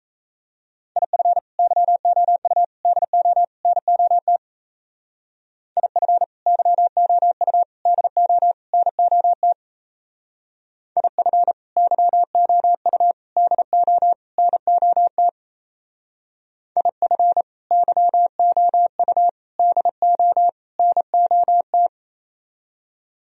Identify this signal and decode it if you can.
Morse code, heard as a keyed tone, about 33 wpm, 700 Hz. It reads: IF YOU DO NOT IF YOU DO NOT IF YOU DO NOT IF YOU DO NOT
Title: Speed-Racing - Sets of 4 Words 28wpm